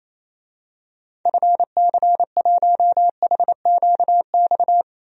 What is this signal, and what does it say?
Morse code, heard as a keyed tone, about 28 wpm, 700 Hz. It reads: FC1HQX
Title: DXCC-100 Call Signs - 6 Characters - 28wpm